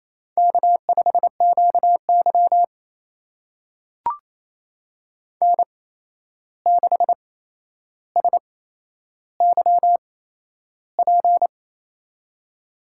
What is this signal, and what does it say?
Morse code, heard as a keyed tone, about 27 wpm, spaced as for 11 wpm, 700 Hz. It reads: K5QY N6SYP
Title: USA Call Signs - 8x Character Spacing - 28wpm